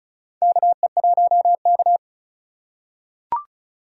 Morse code, heard as a keyed tone, about 36 wpm, 700 Hz. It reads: KE1K
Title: USA Call Signs - 3x Character Spacing - 35wpm